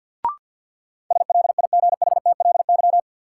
Morse code, heard as a keyed tone, about 53 wpm, 700 Hz. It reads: SLIGHTLY
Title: Sentences from Top 1300 Words - New Word Drill Alt - 50wpm